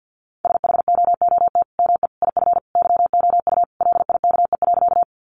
Morse code, heard as a keyed tone, about 50 wpm, 700 Hz. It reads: SHOOT ME IF YOU LIKE.
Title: Sentences from Top 1200 Words 50wpm